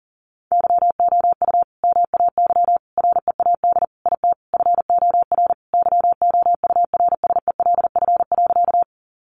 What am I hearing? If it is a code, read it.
Morse code, heard as a keyed tone, about 40 wpm, 700 Hz. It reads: YOU MAY READ IT FOR YOURSELF.
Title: Sentences from Top 600 Words 40wpm